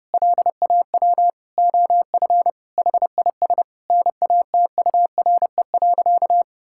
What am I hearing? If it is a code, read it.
Morse code, heard as a keyed tone, about 30 wpm, 700 Hz. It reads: LAW OF HIS NATURE.
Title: Sentences from Top 1000 Words - Review Part 5 - 30wpm